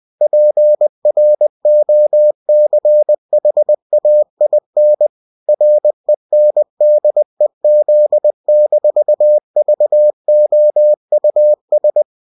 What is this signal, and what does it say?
Morse code, heard as a keyed tone, about 20 wpm, 600 Hz. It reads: PROCHAIN RENDEZ-VOUS